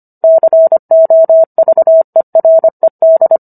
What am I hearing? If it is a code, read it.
Morse code, heard as a keyed tone, about 25 wpm, 650 Hz. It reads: COVERED